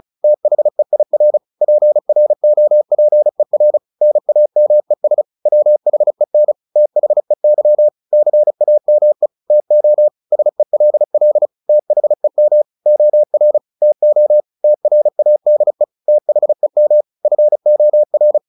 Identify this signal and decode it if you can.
Morse code, heard as a keyed tone, about 34 wpm, 600 Hz. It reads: THEIR PROPER NAMES WHEN THEY CAME TO SELL THEM OR TO TRADE THEM FOR
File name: the old man and the sea / the old man and the sea 1 35wpm